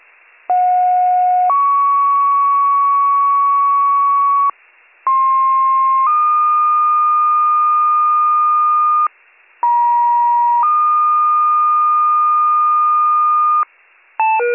Two-tone page, heard as a one-second tone, then a three-second tone, about 730 then 1120 Hz, about 1060 then 1220 Hz, about 950 then 1220 Hz. It is heard from 0.5 to 13.6 s.